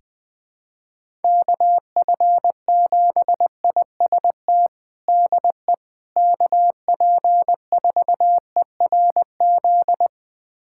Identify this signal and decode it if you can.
Morse code, heard as a keyed tone, about 20 wpm, 700 Hz. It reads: KF7IST DE KP4ERZ
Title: USA Callsign DE Callsign - 6 Characters - 20wpm